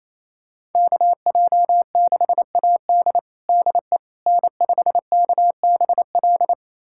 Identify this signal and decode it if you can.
Morse code, heard as a keyed tone, about 28 wpm, 700 Hz. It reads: KJ6AD DE N5KBL
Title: USA Callsign DE Callsign - 5 Characters - 28wpm